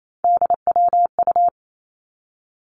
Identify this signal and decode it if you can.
Morse code, heard as a keyed tone, about 28 wpm, 700 Hz. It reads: DWU
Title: Three Challenging Letter-Number Combinations 28wpm